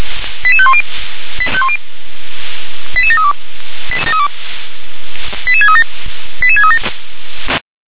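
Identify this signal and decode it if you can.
Five-tone selective call (ZVEI), heard as quick runs of five tones, tones of about 70 ms. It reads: 88510 88732 88637 88637
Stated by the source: ZVEI five-tone selective call received with an SDR.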